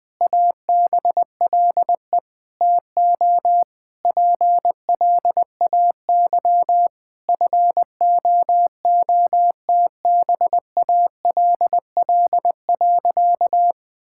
Morse code, heard as a keyed tone, about 20 wpm, 700 Hz. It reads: ABLE TO PLAY FOOTBALL.
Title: Sentences from Top 1600 Words 20wpm